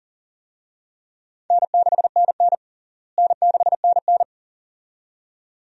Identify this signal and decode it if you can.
Morse code, heard as a keyed tone, about 40 wpm, 700 Hz. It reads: N6NN N6NN
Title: USA Call Signs - 4 Characters - Repeated 2x - 40wpm